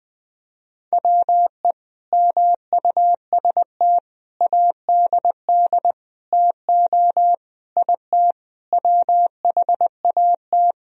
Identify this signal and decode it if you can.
Morse code, heard as a keyed tone, about 20 wpm, 700 Hz. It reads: WE MUST ADD TO IT WHAT
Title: Sentences from Top 1000 Words - Review Part 1 - 20wpm